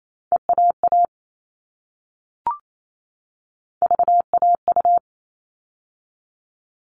Morse code, heard as a keyed tone, about 28 wpm, 700 Hz. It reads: EAA VAU
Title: All Cut Numbers - 3-Digit 28wpm